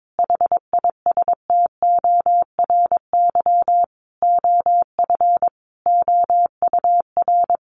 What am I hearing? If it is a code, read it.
Morse code, heard as a keyed tone, about 22 wpm, 700 Hz. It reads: HISTORY OF OUR
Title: Sentences from Top 1500 Words 22wpm